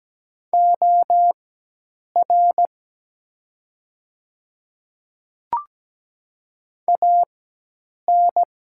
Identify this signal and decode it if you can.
Morse code, heard as a keyed tone, about 17 wpm, spaced as for 8 wpm, 700 Hz. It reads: OR AN
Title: 4x Character Spacing - Top 300 Words - 17wpm